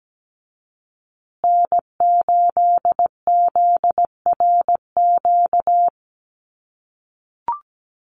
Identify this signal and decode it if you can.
Morse code, heard as a keyed tone, about 17 wpm, 700 Hz. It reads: N8ZRQ
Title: USA Call Signs Repeated 3x 17wpm